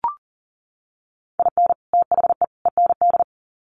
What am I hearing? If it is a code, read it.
Morse code, heard as a keyed tone, about 40 wpm, 700 Hz. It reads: IN THE END
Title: Sets of 3 Words 40wpm